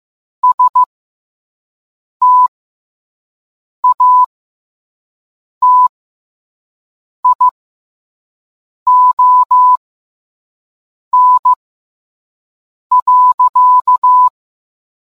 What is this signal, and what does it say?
Morse code, heard as a keyed tone, about 15 wpm, spaced as for 5 wpm, 1000 Hz. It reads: STATION.